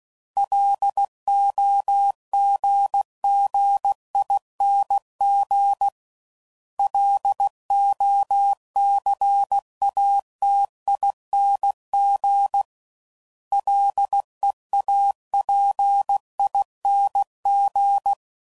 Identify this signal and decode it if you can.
Morse code, heard as a keyed tone, about 16 wpm, 800 Hz. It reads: LOGGING LOCATING LEAPING
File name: Suffixes / Special INGs / Sngl Ings